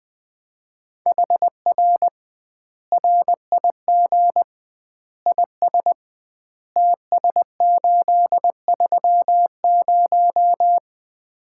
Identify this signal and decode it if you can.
Morse code, heard as a keyed tone, about 21 wpm, 700 Hz. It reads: HR RIG IS TS830
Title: Common QSO Phrases - 2x Word-Spacing - 20wpm